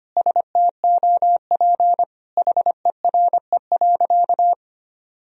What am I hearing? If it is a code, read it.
Morse code, heard as a keyed tone, about 25 wpm, 700 Hz. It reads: STOP HERE.
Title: Sentences from Top 1000 Words - Review Part 2 - 25wpm